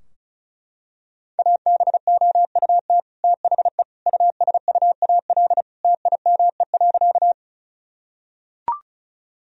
Morse code, heard as a keyed tone, about 36 wpm, 700 Hz. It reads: ABOUT THE USUAL TIME.
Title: Sentences from Top 1000 Words - Review Part 2 - 35wpm